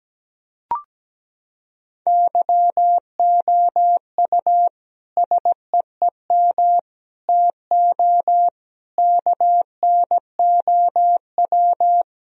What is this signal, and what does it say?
Morse code, heard as a keyed tone, about 17 wpm, 700 Hz. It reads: YOU SEEM TO KNOW
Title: Sentences from Top 1000 Words - Review Part 1 - 17wpm